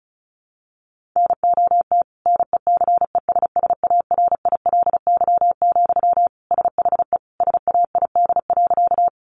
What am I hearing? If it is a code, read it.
Morse code, heard as a keyed tone, about 34 wpm, 700 Hz. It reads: NOT NECESSARILY, SHE SAID.
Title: Sentences from Top 1400 Words 35wpm